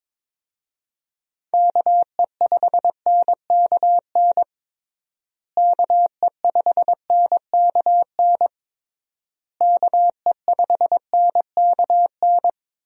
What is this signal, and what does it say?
Morse code, heard as a keyed tone, about 22 wpm, 700 Hz. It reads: KE5NKN KE5NKN KE5NKN
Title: USA Call Signs - 6 Characters - Repeated 3x - 22wpm